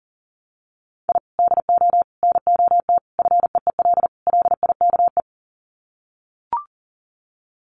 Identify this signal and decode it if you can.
Morse code, heard as a keyed tone, about 40 wpm, 700 Hz. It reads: I DO NOT FEEL LIKE
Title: Sets of 5 Words Encore 40wpm